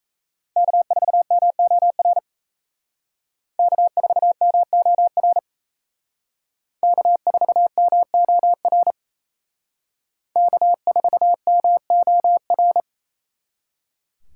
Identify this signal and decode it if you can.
Morse code, heard as a keyed tone, about 35 wpm, 700 Hz. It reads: K4MOR K4MOR K4MOR K4MOR
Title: Speed Racing - USA Call Signs 28wpm